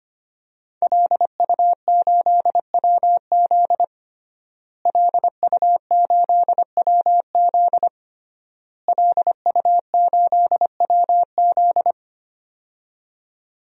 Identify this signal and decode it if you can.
Morse code, heard as a keyed tone, about 25 wpm, 700 Hz. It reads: LU8WZ LU8WZ LU8WZ
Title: DXCC-100 Call Signs - 5 Characters - Repeated 3x - 25wpm